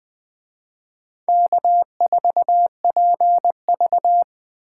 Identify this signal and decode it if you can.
Morse code, heard as a keyed tone, about 20 wpm, 700 Hz. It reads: K4PV